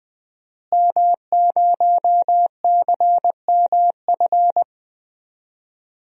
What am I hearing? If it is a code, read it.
Morse code, heard as a keyed tone, about 20 wpm, 700 Hz. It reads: M0CMF
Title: DXCC-100 Call Signs - 5 Characters - Repeated 2x - 20wpm